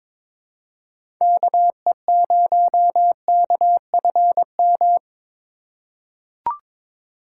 Morse code, heard as a keyed tone, about 22 wpm, 700 Hz. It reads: KE0KFM
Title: USA Call Signs - 4x Character Spacing - 22wpm